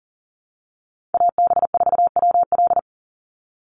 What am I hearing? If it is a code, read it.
Morse code, heard as a keyed tone, about 40 wpm, 700 Hz. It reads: AB4WL
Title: NA SOTA Chaser Callsigns 40wpm